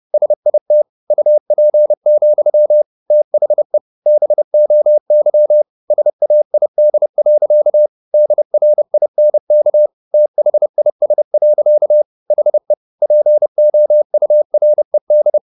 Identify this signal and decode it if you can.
Morse code, heard as a keyed tone, about 30 wpm, 600 Hz. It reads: SIT UP, THE BOY SAID. DRINK THIS. HE POURED